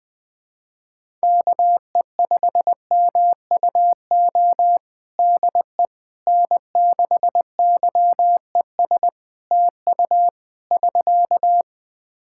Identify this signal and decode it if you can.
Morse code, heard as a keyed tone, about 20 wpm, 700 Hz. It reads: KE5MUO DE N6YES TU <SK>
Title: K1USN SST QSO Practice 20wpm